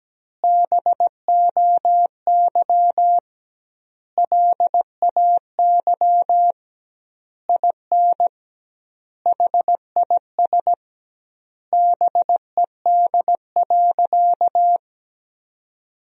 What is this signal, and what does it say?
Morse code, heard as a keyed tone, about 17 wpm, 700 Hz. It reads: BOY LAY IN HIS BED.
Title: 2x Word-Spacing - Sentences from Top 500 Words - 17wpm